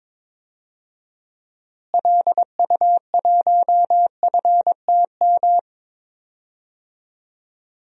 Morse code, heard as a keyed tone, about 22 wpm, 700 Hz. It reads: LU1FTM
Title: DXCC-100 Call Signs - 6 Characters - Repeated 3x - 22wpm